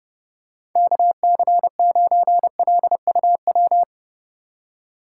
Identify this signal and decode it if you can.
Morse code, heard as a keyed tone, about 32 wpm, 700 Hz. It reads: KC9LUW
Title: USA Call Signs - 6 Characters - 4x Character Spacing - 30wpm